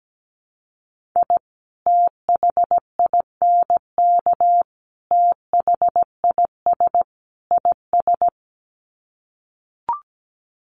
Morse code, heard as a keyed tone, about 17 wpm, 700 Hz. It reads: I THINK THIS IS
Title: Sets of 4 Words 17wpm